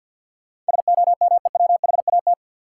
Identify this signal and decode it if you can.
Morse code, heard as a keyed tone, about 50 wpm, 700 Hz. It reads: SOMEWHAT
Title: Sentences from Top 1300 Words - New Word Drill Alt - 50wpm